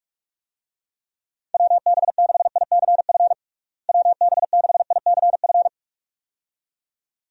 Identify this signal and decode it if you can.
Morse code, heard as a keyed tone, about 41 wpm, 700 Hz. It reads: WB6ICF WB6ICF
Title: USA Call Signs - 6 Characters - Repeated 2x - 45wpm